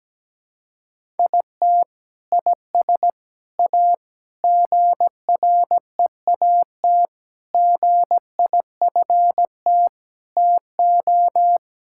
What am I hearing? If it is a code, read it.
Morse code, heard as a keyed tone, about 17 wpm, 700 Hz. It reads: IT IS A GREAT GIFT TO
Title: Sentences from Top 1000 Words - Review Part 6 - 17wpm